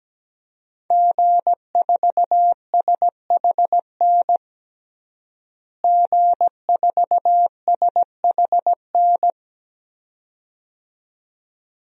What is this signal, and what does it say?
Morse code, heard as a keyed tone, about 17 wpm, 700 Hz. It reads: G4SHN G4SHN
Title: DXCC-100 Call Signs - 5 Characters - Repeated 2x - 17wpm